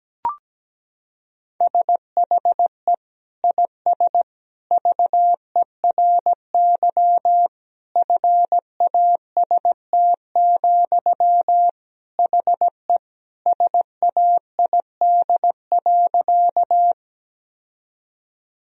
Morse code, heard as a keyed tone, about 17 wpm, 700 Hz. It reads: SHE IS VERY FAST, HE SAID.
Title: Sentences from Top 1000 Words - Review Part 2 - 17wpm